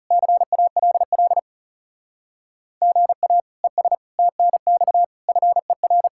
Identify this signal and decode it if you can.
Morse code, heard as a keyed tone, about 35 wpm, 700 Hz. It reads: CALL GA ES TNX FER
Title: Speed-Racing Common QSO Phrases - 30wpm